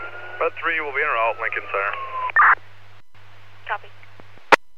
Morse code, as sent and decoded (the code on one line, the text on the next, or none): none